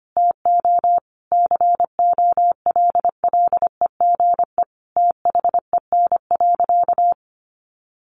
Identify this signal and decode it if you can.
Morse code, heard as a keyed tone, about 25 wpm, 700 Hz. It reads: TO COLLEGE THEN.
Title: Sentences from Top 1300 Words 25wpm